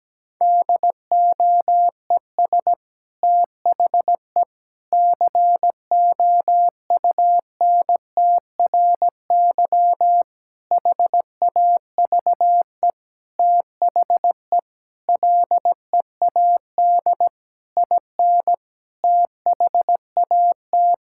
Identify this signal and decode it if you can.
Morse code, heard as a keyed tone, about 17 wpm, 700 Hz. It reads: DOES THE COUNTRY HAVE THE LEAD IN THAT
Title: Sentences from Top 1000 Words - Review Part 5 - 17wpm